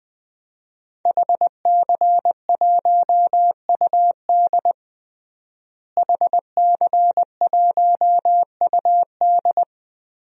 Morse code, heard as a keyed tone, about 21 wpm, 700 Hz. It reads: HC1UD HC1UD
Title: DXCC-100 Call Signs - 5 Characters - Repeated 3x - 20wpm